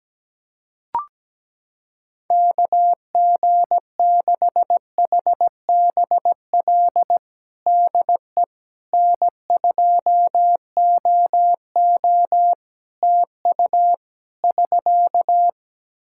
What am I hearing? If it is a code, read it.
Morse code, heard as a keyed tone, about 17 wpm, 700 Hz. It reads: KG6HBL DE N2OO TU <SK>
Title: K1USN SST QSO Practice 17wpm